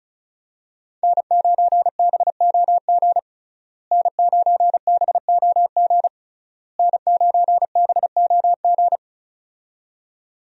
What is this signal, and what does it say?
Morse code, heard as a keyed tone, about 36 wpm, 700 Hz. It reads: N9BOG N9BOG N9BOG
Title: USA Call Signs - 5 Characters - Repeated 3x - 35wpm